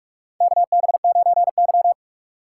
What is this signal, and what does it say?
Morse code, heard as a keyed tone, about 46 wpm, 690 Hz. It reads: KB9Y
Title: USA Call Signs - 6x Character Spacing - 45wpm